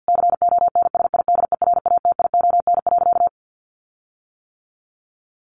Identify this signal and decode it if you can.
Morse code, heard as a keyed tone, about 50 wpm, 700 Hz. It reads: CONSIDERATION.
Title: Sentences from Top 1500 Words 50wpm